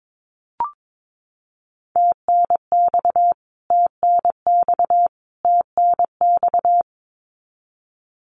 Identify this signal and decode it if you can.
Morse code, heard as a keyed tone, about 22 wpm, 700 Hz. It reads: TNX TNX TNX
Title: Abbreviations Repeated 3x - 22wpm